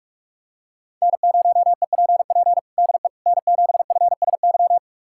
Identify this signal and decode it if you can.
Morse code, heard as a keyed tone, about 41 wpm, 700 Hz. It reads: N0EPP DE N7FSY
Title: USA Callsign DE Callsign - 5 Characters - 45wpm